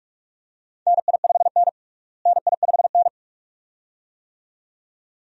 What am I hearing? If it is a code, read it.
Morse code, heard as a keyed tone, about 42 wpm, 700 Hz. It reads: NI5N NI5N
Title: USA Call Signs - 4 Characters - Repeated 2x - 45wpm